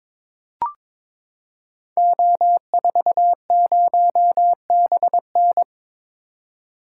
Morse code, heard as a keyed tone, about 22 wpm, 700 Hz. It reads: O40BN
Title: Code Groups - 5 Letter:Numbers - 22wpm